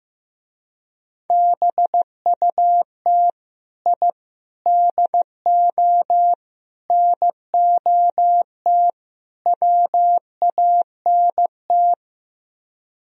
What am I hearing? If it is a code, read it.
Morse code, heard as a keyed tone, about 15 wpm, 700 Hz. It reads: BUT I DO NOT WANT
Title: Speed-Racing - Sets of 5 Words 15wpm